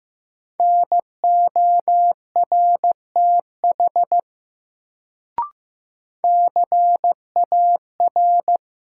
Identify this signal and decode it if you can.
Morse code, heard as a keyed tone, about 15 wpm, 700 Hz. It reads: NORTH CAR
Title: Sets of 5 Words Encore - New Word Drill - 15wpm